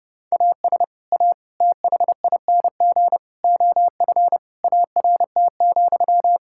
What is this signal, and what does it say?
Morse code, heard as a keyed tone, about 32 wpm, 700 Hz. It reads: AS A THING OF ART,
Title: Sentences from Top 1000 Words - Review Part 6 - 30wpm